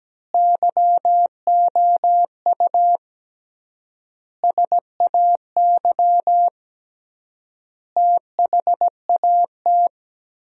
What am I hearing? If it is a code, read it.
Morse code, heard as a keyed tone, about 17 wpm, 700 Hz. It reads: YOU SAY THAT
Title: 3x Word-Spacing - Sentences from Top 200 Words - 17wpm